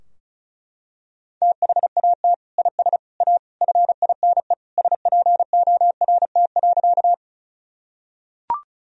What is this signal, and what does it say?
Morse code, heard as a keyed tone, about 34 wpm, 700 Hz. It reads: THAT IS A FINE SPORT.
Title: Sentences from Top 1000 Words - Review Part 4 - 35wpm